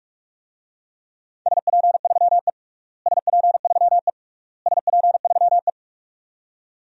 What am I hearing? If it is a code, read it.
Morse code, heard as a keyed tone, about 46 wpm, 700 Hz. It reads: SP3E SP3E SP3E
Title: DXCC-100 Call Signs - 4 Characters - Repeated 3x - 45wpm